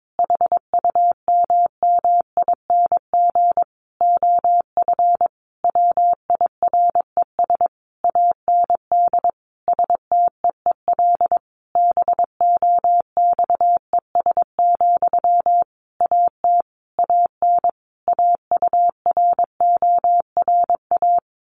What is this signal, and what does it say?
Morse code, heard as a keyed tone, about 22 wpm, 700 Hz. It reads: HUMMING OF WIRES AND STEEL BOXES, AT AN AURORA
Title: A Sound Of Thunder 22wpm